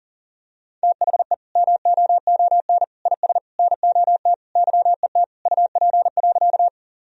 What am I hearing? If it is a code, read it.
Morse code, heard as a keyed tone, about 40 wpm, 700 Hz. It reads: THE MOON IS NOT YET UP.
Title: Sentences from Top 1000 Words - Review Part 3 - 40wpm